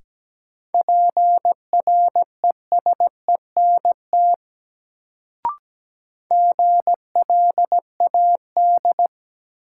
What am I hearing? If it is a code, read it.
Morse code, heard as a keyed tone, about 17 wpm, 700 Hz. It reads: PRESENT GLAD
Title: Sentences from Top 800 Words - New Word Drill - 17wpm